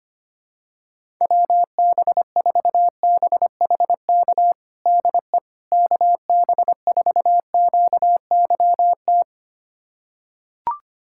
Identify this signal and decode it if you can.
Morse code, heard as a keyed tone, about 25 wpm, 700 Hz. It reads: WB4BHK DE KB4QYT
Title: USA Callsign DE Callsign - 6 Characters - 25wpm